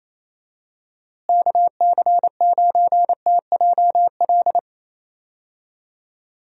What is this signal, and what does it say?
Morse code, heard as a keyed tone, about 27 wpm, 700 Hz. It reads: KC9TJL
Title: USA Call Signs - 6 Characters - Repeated 3x - 28wpm